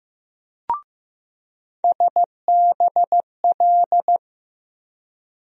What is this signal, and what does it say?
Morse code, heard as a keyed tone, about 15 wpm, 700 Hz. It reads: SBL
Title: USA Call Sign Postfixes - 3 Letters 15wpm